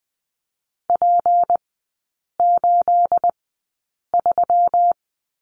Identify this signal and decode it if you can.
Morse code, heard as a keyed tone, about 20 wpm, spaced as for 8 wpm, 690 Hz. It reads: P83
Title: 212 Lesson 34e - Number 8 - Three Characters 20wpm